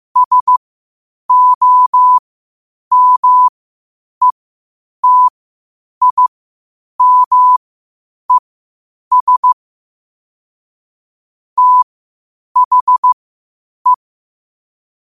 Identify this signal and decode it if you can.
Morse code, heard as a keyed tone, about 15 wpm, spaced as for 8 wpm, 1000 Hz. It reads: SOMETIMES THE
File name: qotdfiles / 8wpm